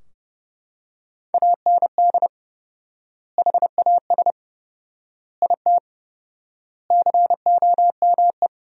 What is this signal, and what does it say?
Morse code, heard as a keyed tone, about 30 wpm, 700 Hz. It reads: AND HAS IT COME
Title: 4x Word-Spacing - Sentences from Top 100 Words - 30wpm